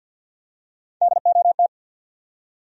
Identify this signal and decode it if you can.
Morse code, heard as a keyed tone, about 53 wpm, 700 Hz. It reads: DOT
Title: USA Common Names - Max 5 Letters - 50wpm